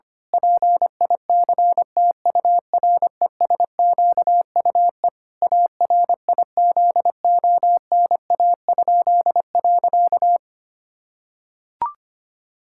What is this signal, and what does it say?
Morse code, heard as a keyed tone, about 25 wpm, 700 Hz. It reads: PICTURESQUE ARIZONA?.
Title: Pangram Challenge v2 - 25wpm